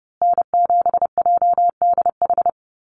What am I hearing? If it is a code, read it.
Morse code, heard as a keyed tone, about 30 wpm, 700 Hz. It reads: N7JDH
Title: USA Call Signs Repeated 2x 30wpm